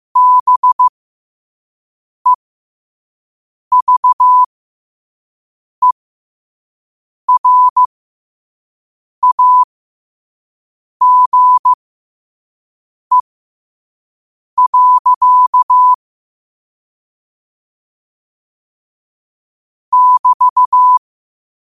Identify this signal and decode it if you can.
Morse code, heard as a keyed tone, about 15 wpm, spaced as for 5 wpm, 1000 Hz. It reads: BEVERAGE. <BT>
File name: qotdfiles / 5wpm